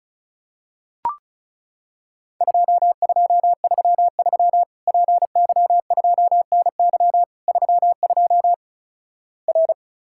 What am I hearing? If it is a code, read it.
Morse code, heard as a keyed tone, about 36 wpm, 700 Hz. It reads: 2233 PY2NY 32 R
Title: Worked All Europe DX Contest - QTC Exchange - 35wpm